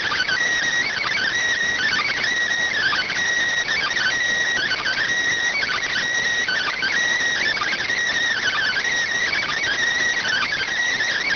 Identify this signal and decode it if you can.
Emergency Alert System (SAME) tones (header bursts), heard as harsh buzzing bursts. It is heard from 0.9 to 10.7 s.